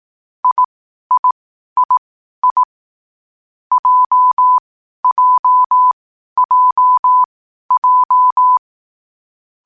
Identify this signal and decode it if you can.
Morse code, heard as a keyed tone, about 18 wpm, spaced as for 12 wpm, 1000 Hz. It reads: IIII JJJJ
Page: Hier findet Ihr das Morsealphabet in langsamer Geschwindigkeit zum Hören. Dabei ist jeder Buchstabe vier Mal hintereinander aufgenommen worden, danach kommt direkt der nächste.